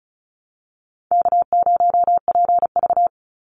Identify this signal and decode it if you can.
Morse code, heard as a keyed tone, about 36 wpm, 700 Hz. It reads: K0PV
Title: Call Signs - 35wpm